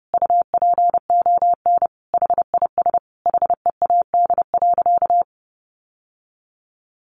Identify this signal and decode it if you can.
Morse code, heard as a keyed tone, about 30 wpm, 700 Hz. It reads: UPON HIS HEAD.
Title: Sentences from Top 1100 Words 30wpm